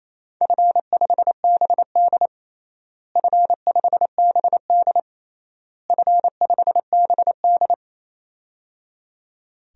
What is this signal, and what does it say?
Morse code, heard as a keyed tone, about 28 wpm, 700 Hz. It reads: F5BD F5BD F5BD
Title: DXCC-100 Call Signs - 4 Characters - Repeated 3x - 28wpm